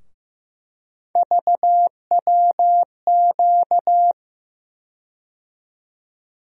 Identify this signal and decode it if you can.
Morse code, heard as a keyed tone, about 15 wpm, 700 Hz. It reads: VWQ
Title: USA Call Sign Postfixes - 3 Letters 15wpm